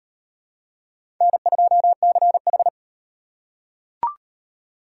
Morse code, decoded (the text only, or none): N2CH